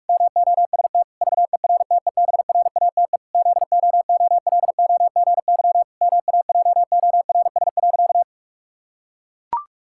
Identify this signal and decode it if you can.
Morse code, heard as a keyed tone, about 41 wpm, 700 Hz. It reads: MOST VERTEBRATE ZOOLOGY MAJORS.
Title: Pangram Challenge v2 - 45wpm